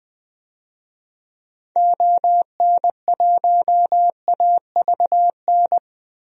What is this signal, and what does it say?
Morse code, heard as a keyed tone, about 20 wpm, 700 Hz. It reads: ON1AVN
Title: DXCC-100 Call Signs - 6 Characters - 20wpm